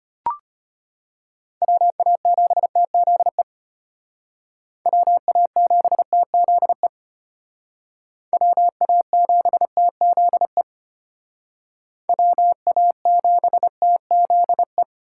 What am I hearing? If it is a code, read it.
Morse code, heard as a keyed tone, about 32 wpm, 700 Hz. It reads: WA7TZE WA7TZE WA7TZE WA7TZE